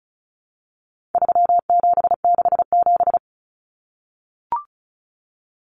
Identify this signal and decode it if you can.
Morse code, heard as a keyed tone, about 34 wpm, 700 Hz. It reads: 3767
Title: Numbers - 4-Digit 35wpm